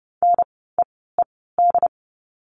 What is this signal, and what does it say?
Morse code, heard as a keyed tone, about 30 wpm, spaced as for 17 wpm, 700 Hz. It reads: NEED
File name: Lesson 16f - Letter c - Words 30wpm